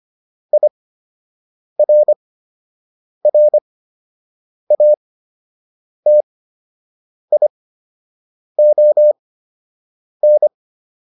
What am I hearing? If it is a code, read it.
Morse code, heard as a keyed tone, about 25 wpm, spaced as for 7 wpm, 600 Hz. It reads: IRRATION